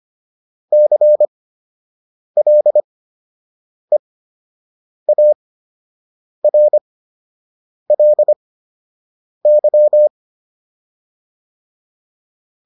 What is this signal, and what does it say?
Morse code, heard as a keyed tone, about 25 wpm, spaced as for 7 wpm, 600 Hz. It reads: CLEARLY